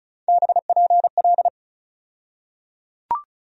Morse code, heard as a keyed tone, about 36 wpm, 700 Hz. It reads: BPL
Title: USA Call Sign Postfixes - 3 Letters 35wpm